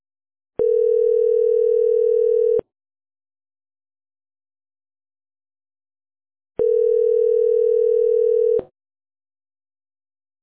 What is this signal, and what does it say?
Ringback tone (North America), 440 + 480 Hz.